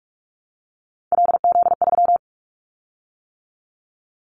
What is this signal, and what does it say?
Morse code, heard as a keyed tone, about 46 wpm, 700 Hz. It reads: LZ3
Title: Callsign Prefixes for Top 100 DXCC - 45wpm